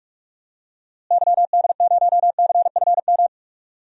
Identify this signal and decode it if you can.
Morse code, heard as a keyed tone, about 46 wpm, 700 Hz. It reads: YD0CFM
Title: DXCC-100 Call Signs - 6 Characters - Repeated 3x - 45wpm